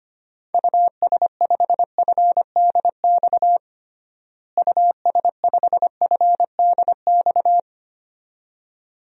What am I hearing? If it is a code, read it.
Morse code, heard as a keyed tone, about 25 wpm, 700 Hz. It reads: US5FDX US5FDX
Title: DXCC-100 Call Signs - 6 Characters - Repeated 2x - 25wpm